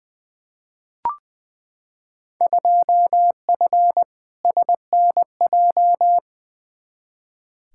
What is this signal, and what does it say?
Morse code, heard as a keyed tone, about 20 wpm, 700 Hz. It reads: 2F SNJ
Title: Field Day Exchanges Encore - 20wpm